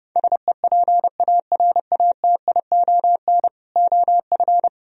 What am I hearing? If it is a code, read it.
Morse code, heard as a keyed tone, about 28 wpm, 700 Hz. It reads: SEPARATION OF